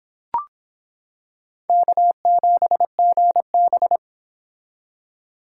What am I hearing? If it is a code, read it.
Morse code, heard as a keyed tone, about 26 wpm, 700 Hz. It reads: K7GB